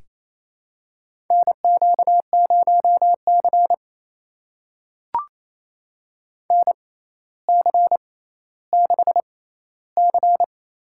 Morse code, heard as a keyed tone, about 29 wpm, spaced as for 15 wpm, 700 Hz. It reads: NQ0C NC6C
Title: USA Call Signs - 4 Characters - 6x Character Spacing - 28wpm